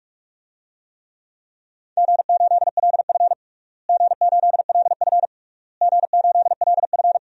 Morse code, heard as a keyed tone, about 46 wpm, 700 Hz. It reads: G8LF G8LF G8LF
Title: DXCC-100 Call Signs - 4 Characters - Repeated 3x - 45wpm